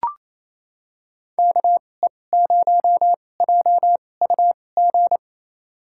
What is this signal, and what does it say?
Morse code, heard as a keyed tone, about 28 wpm, spaced as for 20 wpm, 700 Hz. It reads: KE0JUG
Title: USA Call Signs - 6 Characters - 2x Character Spacing - 28wpm